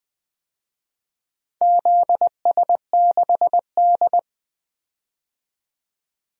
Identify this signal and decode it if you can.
Morse code, heard as a keyed tone, about 20 wpm, 700 Hz. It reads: ZS6D
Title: DXCC-100 Call Signs - 4 Characters - 20wpm